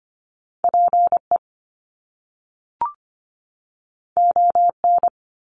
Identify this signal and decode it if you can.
Morse code, heard as a keyed tone, about 25 wpm, 700 Hz. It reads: PE ON
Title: Canadian Provinces and Territories 25wpm